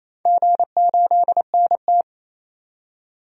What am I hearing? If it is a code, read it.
Morse code, heard as a keyed tone, about 28 wpm, 700 Hz. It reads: G8NT
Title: DXCC-100 Call Signs - 4 Characters - 28wpm